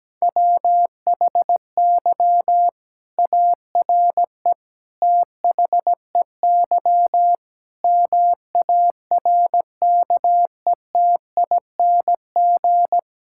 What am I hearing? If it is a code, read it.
Morse code, heard as a keyed tone, about 17 wpm, 700 Hz. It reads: WHY ARE THEY MARKETING
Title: Sentences from Top 1000 Words - Review Part 5 - 17wpm